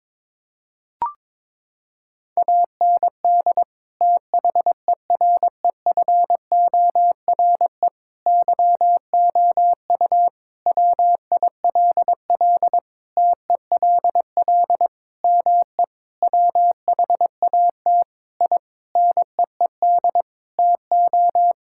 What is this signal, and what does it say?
Morse code, heard as a keyed tone, about 21 wpm, 700 Hz. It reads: AND THEREFORE YOU WILL TELL ME WHAT I NEED TO
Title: Sentences from Top 1000 Words - Review Part 5 - 22wpm